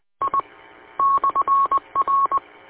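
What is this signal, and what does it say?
Morse code, heard as a keyed tone, about 20 wpm, 1100 Hz. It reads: I /R
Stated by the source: Morse from the W8UM 145.230 audio archive.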